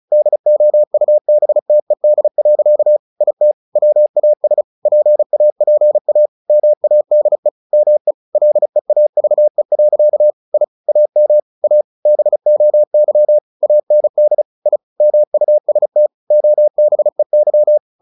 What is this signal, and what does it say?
Morse code, heard as a keyed tone, about 34 wpm, 600 Hz. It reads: DOUBTED. IT WAS PAPA MADE ME LEAVE. I AM A BOY AND I MUST OBEY
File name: the old man and the sea / the old man and the sea 1 35wpm